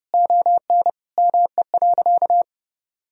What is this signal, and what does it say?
Morse code, heard as a keyed tone, about 30 wpm, 700 Hz. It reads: ON ME.
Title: Sentences from Top 1000 Words - Review Part 2 - 30wpm